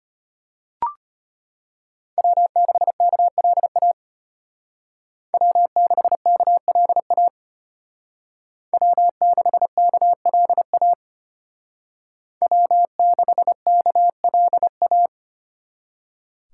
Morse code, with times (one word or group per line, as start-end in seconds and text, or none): 2.2-3.9 W6KLA
5.3-7.3 W6KLA
8.7-10.9 W6KLA
12.4-15.1 W6KLA